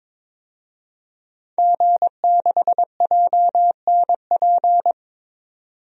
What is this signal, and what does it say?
Morse code, heard as a keyed tone, about 22 wpm, 700 Hz. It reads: G6JNP
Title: DXCC-100 Call Signs - 5 Characters - 22wpm